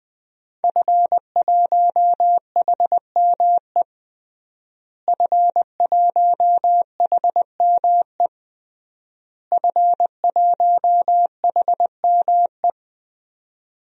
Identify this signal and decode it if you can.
Morse code, heard as a keyed tone, about 21 wpm, 700 Hz. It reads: F1HME F1HME F1HME
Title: DXCC-100 Call Signs - 5 Characters - Repeated 3x - 20wpm